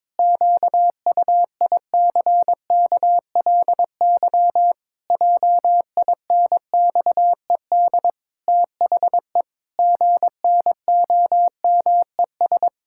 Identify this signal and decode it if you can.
Morse code, heard as a keyed tone, about 22 wpm, 700 Hz. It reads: QUICKLY JINXED THE GNOMES
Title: Pangram Challenge v2 - 22wpm